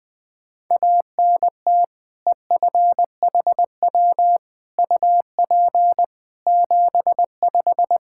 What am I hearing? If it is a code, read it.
Morse code, heard as a keyed tone, about 21 wpm, 700 Hz. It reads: ANT EFHW UP 75
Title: K1USN SST QSO Practice 20wpm